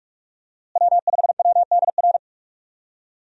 Morse code, heard as a keyed tone, about 41 wpm, 700 Hz. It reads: W5WDR
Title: USA Call Signs - 5 Characters - Repeated 2x - 45wpm